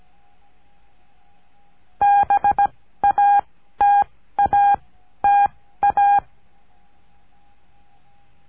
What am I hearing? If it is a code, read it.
Morse code, heard as a keyed tone, about 17 wpm, spaced as for 13 wpm, 810 Hz. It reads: BATATA
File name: cursos / CURSO DE TELEGRAFIA / Telegrafia - Trilha 14